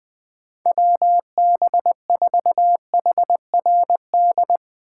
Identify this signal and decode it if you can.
Morse code, heard as a keyed tone, about 20 wpm, 700 Hz. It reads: WB4HRD
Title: USA Call Signs - 6 Characters - 8x Character Spacing - 20wpm